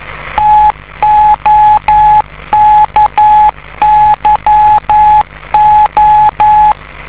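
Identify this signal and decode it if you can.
Morse code, heard as a keyed tone, about 11 wpm, 850 Hz. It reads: TOKYO
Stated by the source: 50 MHz beacon located in Tokyo, Japan.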